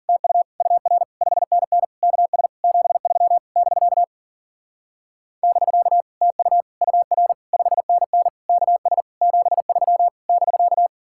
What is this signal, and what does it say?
Morse code, heard as a keyed tone, about 51 wpm, 700 Hz. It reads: TU UR 5NN KS 73 <BK> <BK> TU UR 5NN KS 73 <BK>